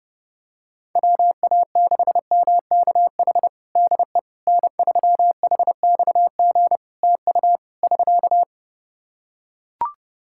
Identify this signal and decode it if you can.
Morse code, heard as a keyed tone, about 30 wpm, 700 Hz. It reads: WA6MKH DE N3HXG TU <SK>
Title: Practice QSO - Exchange 4 - Ending - 30wpm